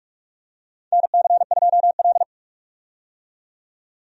Morse code, heard as a keyed tone, about 45 wpm, 700 Hz. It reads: NC2L
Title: USA Call Signs - 4 Characters - 45wpm